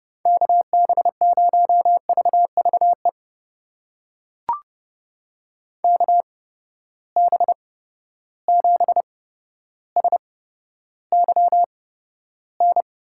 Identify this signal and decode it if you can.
Morse code, heard as a keyed tone, about 31 wpm, spaced as for 13 wpm, 700 Hz. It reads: KB0VVE KB7SYN
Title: USA Call Signs - 6 Characters - 8x Character Spacing - 30wpm